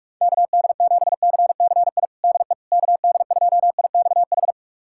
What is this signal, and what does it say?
Morse code, heard as a keyed tone, about 46 wpm, 690 Hz. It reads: KD7CCI DE KD2IXH
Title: USA Callsign DE Callsign - 6 Characters - 45wpm